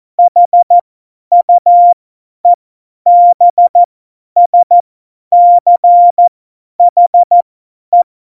Morse code, heard as a keyed tone, about 14 wpm, spaced as for 10 wpm, 700 Hz. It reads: HUEBSCHE
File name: Morsekurs / Witze / Witz 01. Dezember 2020, 50 F70